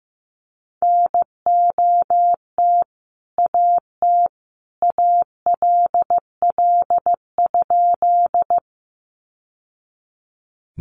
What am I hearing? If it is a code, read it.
Morse code, heard as a keyed tone, about 15 wpm, 700 Hz. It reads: NOT AT ALL?
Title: Sentences from Top 600 Words 15wpm